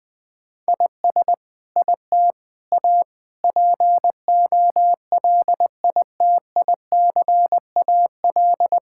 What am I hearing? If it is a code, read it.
Morse code, heard as a keyed tone, about 20 wpm, 700 Hz. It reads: IS IT A POLITICAL
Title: Sentences from Top 1000 Words - Review Part 5 - 20wpm